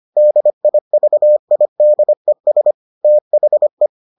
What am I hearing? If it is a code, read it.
Morse code, heard as a keyed tone, about 25 wpm, 600 Hz. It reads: DIVIDES THE